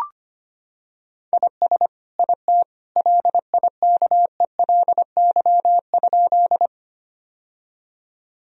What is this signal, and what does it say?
Morse code, heard as a keyed tone, about 25 wpm, 700 Hz. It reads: IS IT LIKELY?
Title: Sentences from Top 1000 Words - Review Part 4 - 25wpm